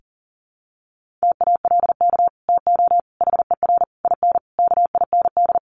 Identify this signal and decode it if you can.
Morse code, heard as a keyed tone, about 37 wpm, 700 Hz. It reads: TALK TO HER IN KIND
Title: Sentences from Top 1500 Words 40wpm